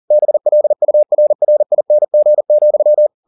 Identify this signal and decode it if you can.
Morse code, heard as a keyed tone, about 40 wpm, 600 Hz. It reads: BLURRING,